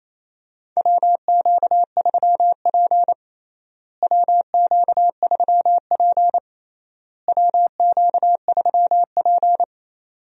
Morse code, heard as a keyed tone, about 28 wpm, 700 Hz. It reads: WQ3P WQ3P WQ3P
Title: USA Call Signs - 4 Characters - Repeated 3x - 28wpm